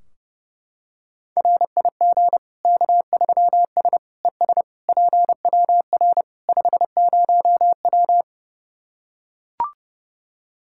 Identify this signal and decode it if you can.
Morse code, heard as a keyed tone, about 30 wpm, 700 Hz. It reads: RIG K3S ES PWR 50W
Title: Practice QSO - Exchange 2 - Info - 30wpm